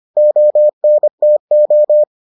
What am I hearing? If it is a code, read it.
Morse code, heard as a keyed tone, about 25 wpm, 600 Hz. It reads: ONTO